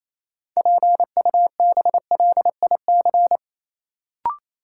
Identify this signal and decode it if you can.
Morse code, heard as a keyed tone, about 28 wpm, 700 Hz. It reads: PUBLIC
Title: Sentences from Top 600 Words - New Word Drill - 28wpm